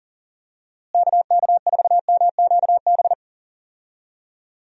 Morse code, heard as a keyed tone, about 40 wpm, 700 Hz. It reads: KK4MQB
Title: USA Call Signs - 6 Characters - Repeated 2x - 40wpm